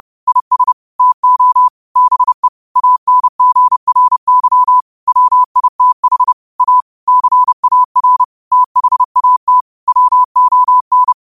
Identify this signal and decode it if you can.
Morse code, heard as a keyed tone, about 30 wpm, 1000 Hz. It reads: IS TO BE ANGRY WITH A CAR THAT WON
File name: qotdfiles / 30wpm